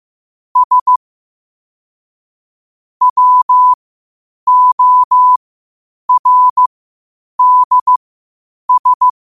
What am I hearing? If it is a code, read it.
Morse code, heard as a keyed tone, about 15 wpm, spaced as for 8 wpm, 1000 Hz. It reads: S WORDS
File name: qotdfiles / 8wpm